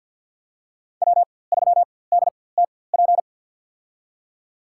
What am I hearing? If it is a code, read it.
Morse code, heard as a keyed tone, about 54 wpm, spaced as for 24 wpm, 700 Hz. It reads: W3DTP
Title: USA Call Signs - 5 Characters - 4x Character Spacing - 50wpm